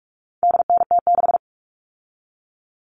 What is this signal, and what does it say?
Morse code, heard as a keyed tone, about 44 wpm, 700 Hz. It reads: DNT6
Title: All Cut Numbers - 4-Digit 45wpm